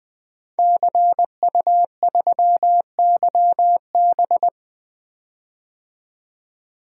Morse code, heard as a keyed tone, about 20 wpm, 700 Hz. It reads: CU3YB
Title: DXCC-100 Call Signs - 5 Characters - 20wpm